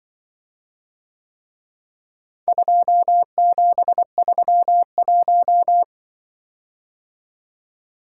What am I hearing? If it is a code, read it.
Morse code, heard as a keyed tone, about 24 wpm, 700 Hz. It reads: 2731
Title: Speed-Racing - Four Letter-Number Combinations - 20wpm